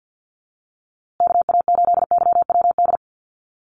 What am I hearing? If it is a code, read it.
Morse code, heard as a keyed tone, about 50 wpm, 700 Hz. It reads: KA8YWD
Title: USA Call Signs Repeated 2x 50wpm